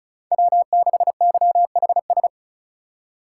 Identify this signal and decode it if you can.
Morse code, heard as a keyed tone, about 34 wpm, 700 Hz. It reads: W6YHS
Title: USA Call Signs - 5 Characters - 2x Character Spacing - 35wpm